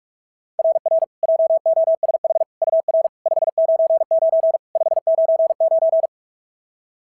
Morse code, heard as a keyed tone, about 48 wpm, 690 Hz. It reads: RR JOSH UR 599 599
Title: SOTA QSO Practice 45wpm